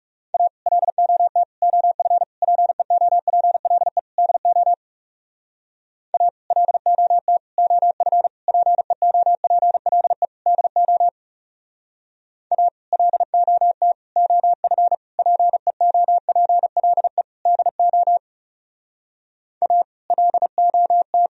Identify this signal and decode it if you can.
Morse code, heard as a keyed tone, about 36 wpm, 700 Hz. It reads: A LOT OF PEOPLE DO A LOT OF PEOPLE DO A LOT OF PEOPLE DO A LOT
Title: Speed-Racing - Sets of 5 Words 30wpm